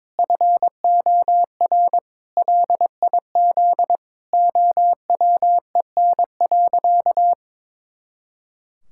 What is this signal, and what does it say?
Morse code, heard as a keyed tone, about 22 wpm, 700 Hz. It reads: FOR LIZ OWEN.
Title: Pangram Challenge v2 - 22wpm